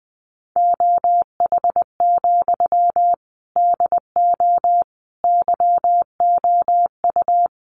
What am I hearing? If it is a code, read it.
Morse code, heard as a keyed tone, about 20 wpm, 700 Hz. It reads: OH, DO YOU
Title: Sentences from Top 500 Words 20wpm